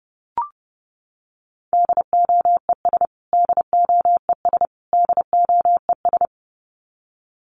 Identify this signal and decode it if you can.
Morse code, heard as a keyed tone, about 30 wpm, 700 Hz. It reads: DOES DOES DOES
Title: Top 300 Words Repeated 30wpm